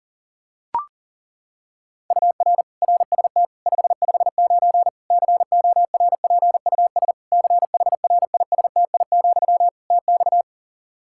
Morse code, heard as a keyed tone, about 40 wpm, 700 Hz. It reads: UR RST 559 CORPUS CHRISTI, TX
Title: Practice QSO - Exchange 1 - Beginning - 40wpm